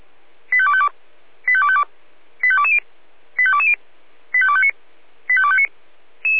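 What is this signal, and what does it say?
Five-tone selective call (ZVEI), heard as quick runs of five tones, tones of about 70 ms. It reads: 85252 85252 85220 85220 85259 85259
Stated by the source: ZVEI five-tone selective call received by SDR.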